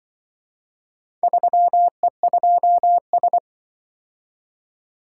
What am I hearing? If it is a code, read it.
Morse code, heard as a keyed tone, about 24 wpm, 700 Hz. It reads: 3E2S
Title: Speed-Racing - Four Letter-Number Combinations - 20wpm